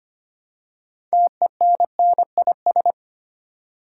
Morse code, heard as a keyed tone, about 25 wpm, 700 Hz. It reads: TENNIS
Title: Six Letter Words - 25wpm